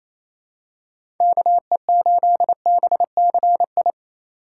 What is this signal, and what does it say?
Morse code, heard as a keyed tone, about 28 wpm, 700 Hz. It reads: KE8BCI
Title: USA Call Signs - 6 Characters - 2x Character Spacing - 28wpm